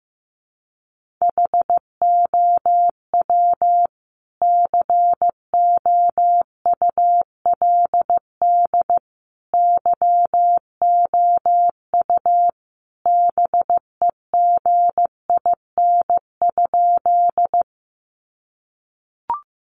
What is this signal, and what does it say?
Morse code, heard as a keyed tone, about 15 wpm, 700 Hz. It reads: HOW COULD YOU BEGIN?
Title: Sentences from Top 300 Words 15wpm